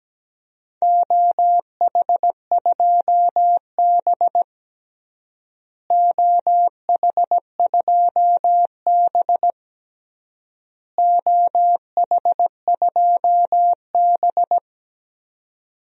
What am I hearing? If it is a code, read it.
Morse code, heard as a keyed tone, about 17 wpm, 700 Hz. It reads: OH2B OH2B OH2B
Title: DXCC-100 Call Signs - 4 Characters - Repeated 3x - 17wpm